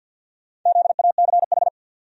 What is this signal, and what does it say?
Morse code, heard as a keyed tone, about 51 wpm, 700 Hz. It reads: ZACH
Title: USA Common Names - 50wpm